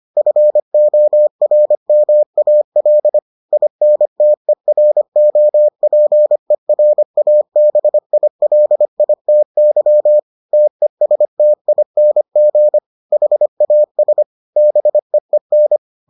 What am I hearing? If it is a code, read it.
Morse code, heard as a keyed tone, about 25 wpm, 600 Hz. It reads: FORMAL INTEROPERABILITY TESTING HAS BEEN